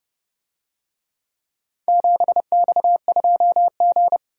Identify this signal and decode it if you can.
Morse code, heard as a keyed tone, about 30 wpm, 700 Hz. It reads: 7X2G